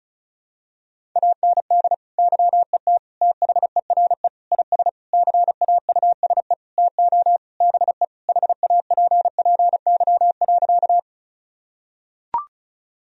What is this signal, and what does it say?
Morse code, heard as a keyed tone, about 34 wpm, 700 Hz. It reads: AND YET THERE IS CAUSE TO BE HAPPY.
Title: Sentences from Top 1000 Words - Review Part 1 - 35wpm